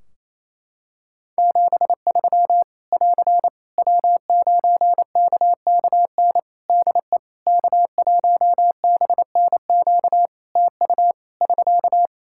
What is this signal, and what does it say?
Morse code, heard as a keyed tone, about 28 wpm, 700 Hz. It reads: 73 <AR> W9KKN DE K1BNQ TU <SK>
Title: K1USN SST QSO Practice 28wpm